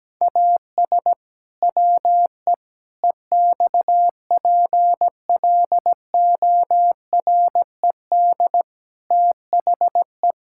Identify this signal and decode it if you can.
Morse code, heard as a keyed tone, about 17 wpm, 700 Hz. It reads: AS WE EXPLORED THE
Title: Pangram Challenge v2 - 17wpm